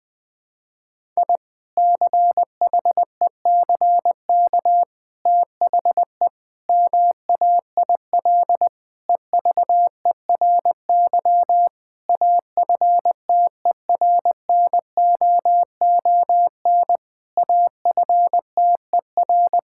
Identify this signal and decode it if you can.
Morse code, heard as a keyed tone, about 20 wpm, 700 Hz. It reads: I CHECK THE MAIL EVERY AFTERNOON AFTER
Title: Difficult Sentences with Hints - 20wpm